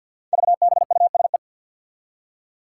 Morse code, heard as a keyed tone, about 50 wpm, 700 Hz. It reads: VBUSE
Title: Five Challenging Letter-Number Combinations 50wpm